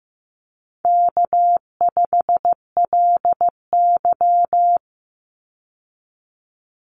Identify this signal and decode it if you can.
Morse code, heard as a keyed tone, about 15 wpm, 700 Hz. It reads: K5LY
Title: Call Signs Repeated 2x - 15wpm